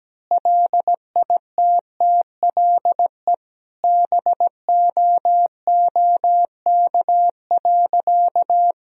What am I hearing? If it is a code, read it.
Morse code, heard as a keyed tone, about 17 wpm, 700 Hz. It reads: LITTLE BOOK.
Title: Sentences from Top 1000 Words - Review Part 2 - 17wpm